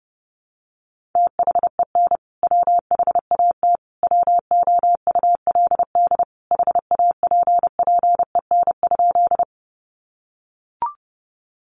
Morse code, heard as a keyed tone, about 28 wpm, 700 Hz. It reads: THEN WHAT WOULD HAPPEN?
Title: Sentences from Top 400 Words 30wpm